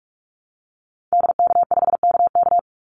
Morse code, heard as a keyed tone, about 44 wpm, 700 Hz. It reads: DK5KK
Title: Call Signs Repeated 2x - 45wpm